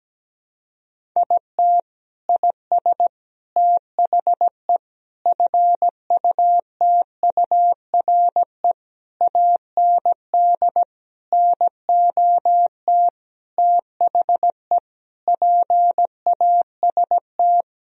Morse code, heard as a keyed tone, about 17 wpm, 700 Hz. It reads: IT IS THE FUTURE AND NOT THE PAST
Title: Sentences from Top 1000 Words - Review Part 4 - 17wpm